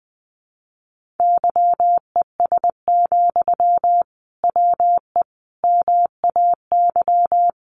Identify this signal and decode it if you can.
Morse code, heard as a keyed tone, about 20 wpm, 700 Hz. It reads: YES, WE MAY
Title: Sentences from Top 1500 Words 20wpm